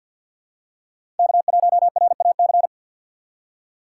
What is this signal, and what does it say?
Morse code, heard as a keyed tone, about 53 wpm, 700 Hz. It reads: K1RAC
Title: USA Call Signs - 5 Characters - 8x Character Spacing - 50wpm